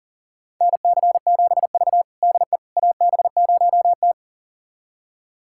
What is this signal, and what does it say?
Morse code, heard as a keyed tone, about 37 wpm, 700 Hz. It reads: NC7V DE AB0T
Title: USA Callsign DE Callsign - 4 Characters - 40wpm